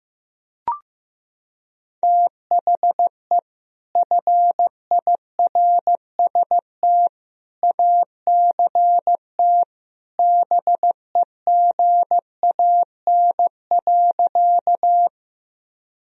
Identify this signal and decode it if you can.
Morse code, heard as a keyed tone, about 15 wpm, 700 Hz. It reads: THE FIRST ACT BEGAN.
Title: Sentences from Top 1000 Words - Review Part 1 - 15wpm